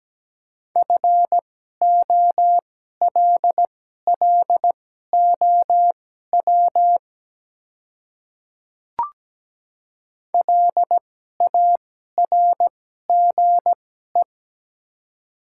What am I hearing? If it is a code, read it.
Morse code, heard as a keyed tone, about 17 wpm, spaced as for 12 wpm, 700 Hz. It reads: FOLLOW LARGE
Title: 2x Character Spacing - Top 300 Words - 17wpm